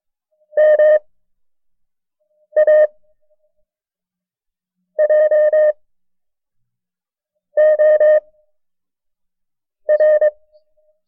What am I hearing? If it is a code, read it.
Morse code, heard as a keyed tone, about 23 wpm, spaced as for 5 wpm, 610 Hz. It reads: MAJOR
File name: FreeFiles / CW-practice / KB6MT-tape1-side2